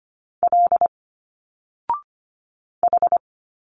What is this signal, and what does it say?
Morse code, heard as a keyed tone, about 24 wpm, 700 Hz. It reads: L H
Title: Single Letters 25wpm